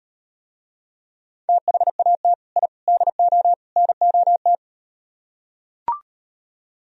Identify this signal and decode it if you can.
Morse code, heard as a keyed tone, about 39 wpm, 700 Hz. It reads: THAT I DO NOT